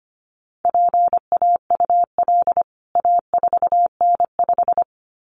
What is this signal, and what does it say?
Morse code, heard as a keyed tone, about 25 wpm, 700 Hz. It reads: PAUL A4N5
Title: CWT Exchange 25wpm